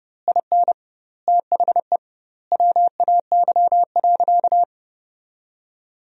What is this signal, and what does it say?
Morse code, heard as a keyed tone, about 30 wpm, 700 Hz. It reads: IN THE WAY.
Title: 2x Word-Spacing - Sentences from Top 400 Words - 30wpm